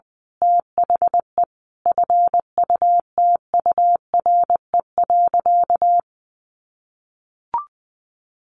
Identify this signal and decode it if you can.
Morse code, heard as a keyed tone, about 20 wpm, 700 Hz. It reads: THE FUTURE.
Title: A Sound Of Thunder 20wpm